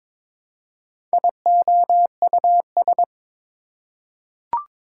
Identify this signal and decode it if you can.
Morse code, heard as a keyed tone, about 22 wpm, 700 Hz. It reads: IOUS
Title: Prefixes and Suffixes 22wpm